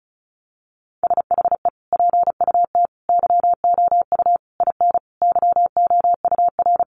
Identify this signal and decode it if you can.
Morse code, heard as a keyed tone, about 34 wpm, 700 Hz. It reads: SHE PUT YOU IN YOUR
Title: Sentences from Top 800 Words 35wpm